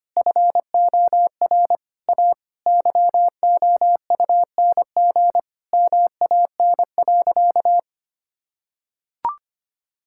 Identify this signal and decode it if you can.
Morse code, heard as a keyed tone, about 25 wpm, 700 Hz. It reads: FOR A YOUNG MAN.
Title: Sentences from Top 1000 Words - Review Part 5 - 25wpm